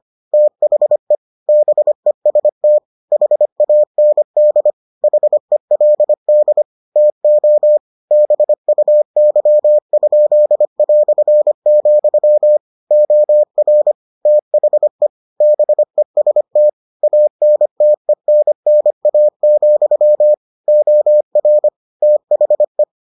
Morse code, heard as a keyed tone, about 25 wpm, 600 Hz. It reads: THE BEST HAND HELD TO BUY?", OR THE BEST ANTENNA, OR THE